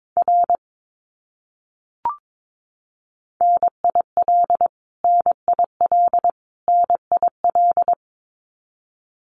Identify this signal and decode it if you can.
Morse code, heard as a keyed tone, about 22 wpm, 700 Hz. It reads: R NIL NIL NIL
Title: Abbreviations Repeated 3x - 22wpm